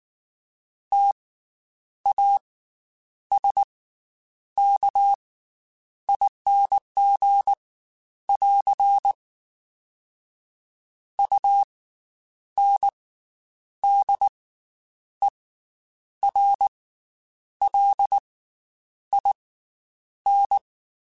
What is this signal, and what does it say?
Morse code, heard as a keyed tone, about 19 wpm, 800 Hz. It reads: T A S K ING <AR> U N D E R L I N
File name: Special INGs / Special INGs-mp3s / Slow ing @18